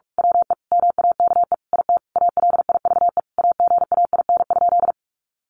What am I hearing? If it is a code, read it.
Morse code, heard as a keyed tone, about 45 wpm, 700 Hz. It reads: WE MAKE IT ALIVE AGAIN?
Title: A Sound Of Thunder 45wpm